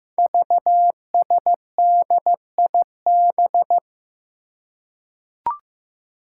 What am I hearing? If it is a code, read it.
Morse code, heard as a keyed tone, about 15 wpm, 700 Hz. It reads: VSDIB
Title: Five Challenging Letter-Number Combinations 15wpm